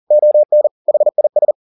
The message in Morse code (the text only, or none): ON HIS